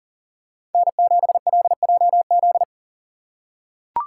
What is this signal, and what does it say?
Morse code, heard as a keyed tone, about 40 wpm, 700 Hz. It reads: N7LJZ
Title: USA Call Signs - 5 Characters - 2x Character Spacing - 40wpm